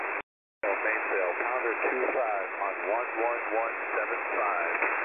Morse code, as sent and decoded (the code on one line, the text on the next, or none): none